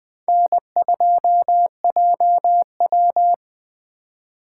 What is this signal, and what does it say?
Morse code, heard as a keyed tone, about 20 wpm, 700 Hz. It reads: N2JW